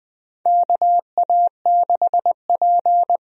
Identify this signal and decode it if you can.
Morse code, heard as a keyed tone, about 20 wpm, 700 Hz. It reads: KA6P
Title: USA Call Signs - 4 Characters - Repeated 2x - 20wpm